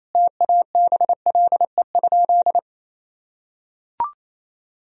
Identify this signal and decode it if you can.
Morse code, heard as a keyed tone, about 28 wpm, 700 Hz. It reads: TABLE?
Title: Sentences from Top 1000 Words - Review Part 2 - 28wpm